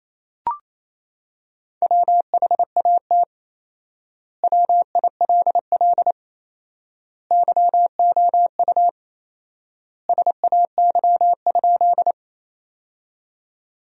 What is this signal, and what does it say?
Morse code, heard as a keyed tone, about 28 wpm, 700 Hz. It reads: WHAT WILL YOU SAY?
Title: 4x Word-Spacing - Sentences from Top 200 Words - 28wpm